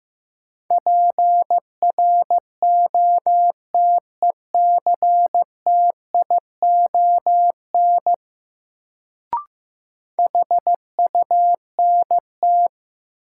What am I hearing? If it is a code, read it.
Morse code, heard as a keyed tone, about 15 wpm, 700 Hz. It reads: PROTECTION HUNT
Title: Sentences from Top 1600 Words - New Word Drill Alt - 15wpm